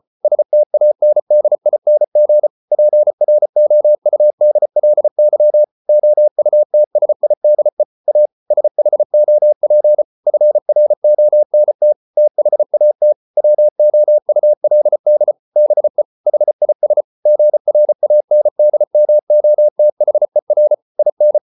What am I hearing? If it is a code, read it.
Morse code, heard as a keyed tone, about 34 wpm, 600 Hz. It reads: STANDING PROUDLY OUTSIDE A SHOP FRONT THAT WOULD BE HIS GRANDMOTHER IN